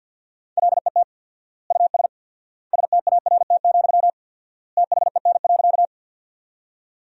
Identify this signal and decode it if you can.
Morse code, heard as a keyed tone, about 50 wpm, 700 Hz. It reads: LET US START, THEN.
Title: 4x Word-Spacing - Sentences from Top 300 Words - 50wpm